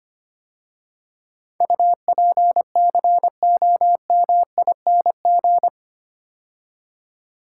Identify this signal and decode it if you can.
Morse code, heard as a keyed tone, about 25 wpm, 700 Hz. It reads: UPCOMING
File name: 8 Letter Words - First 3 Letters Given - 25wpm